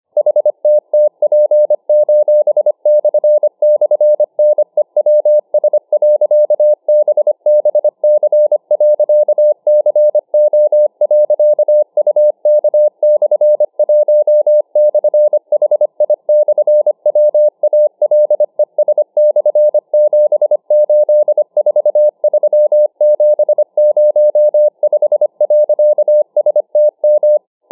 Morse code, heard as a keyed tone, about 25 wpm, 600 Hz. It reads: HTTP://NEWS.BBC.CO.UK/1/HI/WALES/7843705.STM